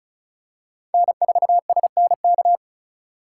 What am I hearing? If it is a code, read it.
Morse code, heard as a keyed tone, about 36 wpm, 700 Hz. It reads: N4SNK
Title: USA Call Signs - 5 Characters - 6x Character Spacing - 35wpm